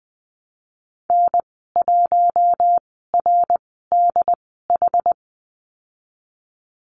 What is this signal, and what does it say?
Morse code, heard as a keyed tone, about 20 wpm, spaced as for 15 wpm, 700 Hz. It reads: N1RDH
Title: 154 Lesson 25g - Letter p - Call Signs 20wpm